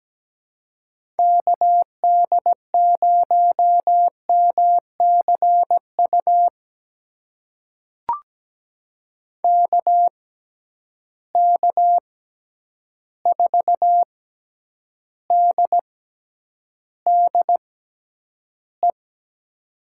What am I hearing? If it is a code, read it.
Morse code, heard as a keyed tone, about 17 wpm, spaced as for 9 wpm, 700 Hz. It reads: KD0MCU KK4DDE
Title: USA Call Signs - 6 Characters - 6x Character Spacing - 17wpm